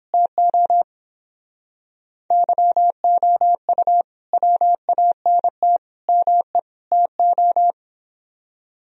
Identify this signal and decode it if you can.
Morse code, heard as a keyed tone, about 29 wpm, 700 Hz. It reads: TO YOU WANT ME TO